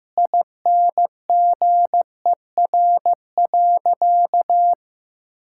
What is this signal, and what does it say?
Morse code, heard as a keyed tone, about 15 wpm, 700 Hz. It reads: INGER.
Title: Sentences from Top 1600 Words 15wpm